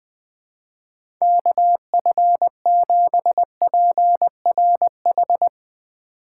Morse code, heard as a keyed tone, about 21 wpm, 700 Hz. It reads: KF7PRH
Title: USA Call Signs - 4x Character Spacing - 20wpm